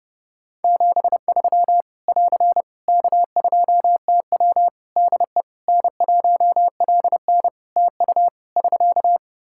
Morse code, heard as a keyed tone, about 32 wpm, 700 Hz. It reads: 73 <AR> K2TW DE N1LN TU <SK>
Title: Practice QSO - Exchange 4 - Ending - 30wpm